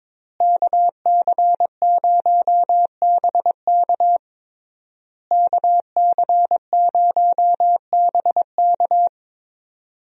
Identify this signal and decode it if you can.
Morse code, heard as a keyed tone, about 22 wpm, 700 Hz. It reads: KC0BK KC0BK
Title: USA Call Signs - 5 Characters - Repeated 3x - 22wpm